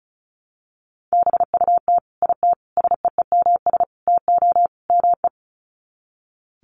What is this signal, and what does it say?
Morse code, heard as a keyed tone, about 34 wpm, 700 Hz. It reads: BUT IT SEEMS TO ME
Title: Sets of 5 Words Encore2 35wpm